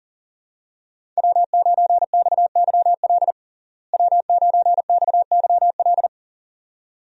Morse code, heard as a keyed tone, about 40 wpm, 700 Hz. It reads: W9XYL W9XYL
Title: USA Call Signs - 5 Characters - Repeated 2x - 40wpm